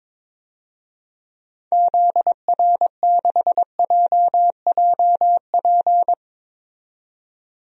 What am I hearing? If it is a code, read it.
Morse code, heard as a keyed tone, about 21 wpm, 700 Hz. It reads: ZR6JJP
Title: DXCC-100 Call Signs - 6 Characters - 22wpm